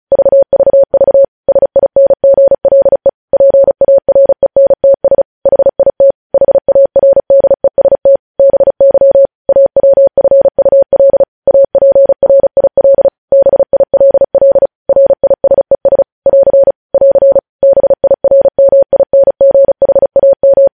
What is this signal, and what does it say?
Morse code, heard as a keyed tone, about 36 wpm, 550 Hz. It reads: VVV SINGLE PARENTS HIT HARDEST BY AWFUL APRIL BILL RISES <AR> <AR> BIRMINGHAM
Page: News Headlines in Morse Code at 35 WPM